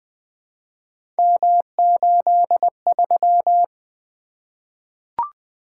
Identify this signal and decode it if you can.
Morse code, heard as a keyed tone, about 20 wpm, 700 Hz. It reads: M83
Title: Three Letter-Number Combinations - 20wpm